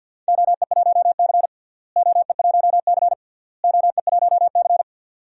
Morse code, heard as a keyed tone, about 53 wpm, 700 Hz. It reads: OE1C OE1C OE1C
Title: DXCC-100 Call Signs - 4 Characters - Repeated 3x - 50wpm